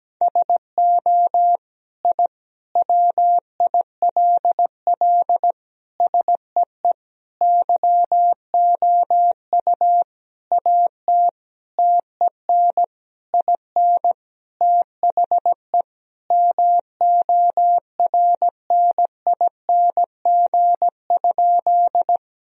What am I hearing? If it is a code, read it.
Morse code, heard as a keyed tone, about 17 wpm, 700 Hz. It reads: SO I WILL SEE YOU AT TEN IN THE MORNING?
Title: Sentences from Top 1000 Words - Review Part 2 - 17wpm